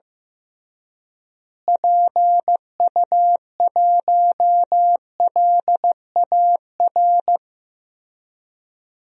Morse code, heard as a keyed tone, about 15 wpm, 700 Hz. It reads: PU1LAR
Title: DXCC-100 Call Signs - 6 Characters - Repeated 3x - 15wpm